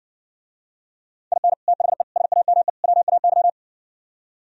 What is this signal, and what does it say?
Morse code, heard as a keyed tone, about 64 wpm, 700 Hz. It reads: IN THE SAME WAY